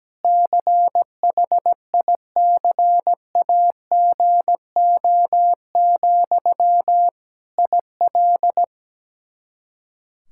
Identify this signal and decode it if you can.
Morse code, heard as a keyed tone, about 17 wpm, 700 Hz. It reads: CHICAGO, IL
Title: K1USN SST QSO Practice 17wpm